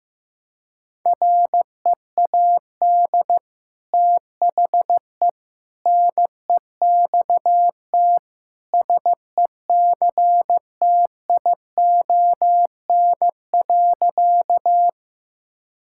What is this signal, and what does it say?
Morse code, heard as a keyed tone, about 15 wpm, 700 Hz. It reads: READ THE NEXT SECTION.
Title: Sentences from Top 1000 Words - Review Part 5 - 15wpm